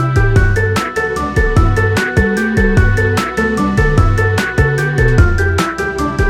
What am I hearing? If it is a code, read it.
There is no code.